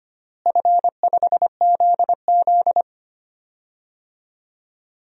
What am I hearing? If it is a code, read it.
Morse code, heard as a keyed tone, about 25 wpm, 700 Hz. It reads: F5ZZ
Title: DXCC-100 Call Signs - 4 Characters - Repeated 3x - 25wpm